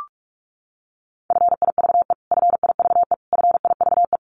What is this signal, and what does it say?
Morse code, heard as a keyed tone, about 44 wpm, 700 Hz. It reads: FIVE FIVE FIVE
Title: Top 500 Words Repeated 45wpm